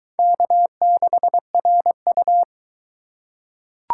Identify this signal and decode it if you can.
Morse code, heard as a keyed tone, about 21 wpm, 700 Hz. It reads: K6RU